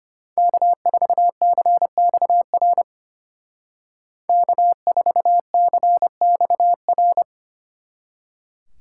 Morse code, heard as a keyed tone, about 33 wpm, 700 Hz. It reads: K4CXR K4CXR